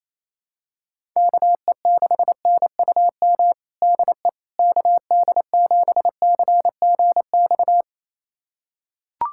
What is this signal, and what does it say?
Morse code, heard as a keyed tone, about 27 wpm, 700 Hz. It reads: KE6NUM DE KD7CGX
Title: USA Callsign DE Callsign - 6 Characters - 28wpm